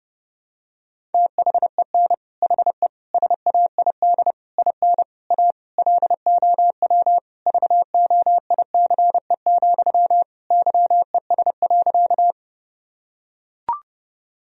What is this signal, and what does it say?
Morse code, heard as a keyed tone, about 30 wpm, 700 Hz. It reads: THEN HE SAID IN A LOW VOICE, YES.
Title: Sentences from Top 1000 Words - Review Part 1 - 30wpm